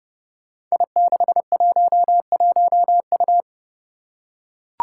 Morse code, heard as a keyed tone, about 30 wpm, 700 Hz. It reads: I611U
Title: Five Challenging Letter-Number Combinations 30wpm